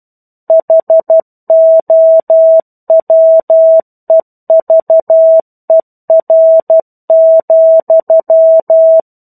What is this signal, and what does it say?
Morse code, heard as a keyed tone, about 12 wpm, 650 Hz. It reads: HOWEVER,
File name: 01-Chapter-01 12 WPM